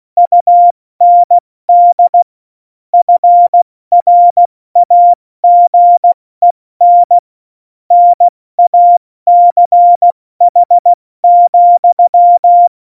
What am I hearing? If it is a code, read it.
Morse code, heard as a keyed tone, about 16 wpm, 700 Hz. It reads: UND FRAGEN NACH,
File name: Morsekurs / Witze / Witz 10.Dezember 2019, 70 F80